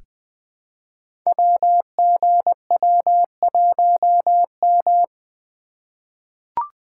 Morse code, heard as a keyed tone, about 20 wpm, 700 Hz. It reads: WGW1M
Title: Code Groups - 5 Letter:Numbers - 20wpm